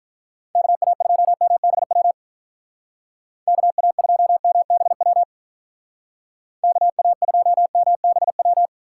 Morse code, heard as a keyed tone, about 43 wpm, 700 Hz. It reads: KA2MBW KA2MBW KA2MBW
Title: Speed Racing - USA 2x3 Call Signs 35wpm